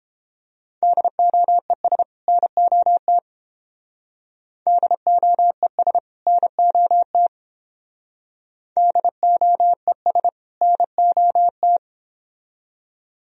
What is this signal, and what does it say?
Morse code, heard as a keyed tone, about 30 wpm, 700 Hz. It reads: DOES NOT DOES NOT DOES NOT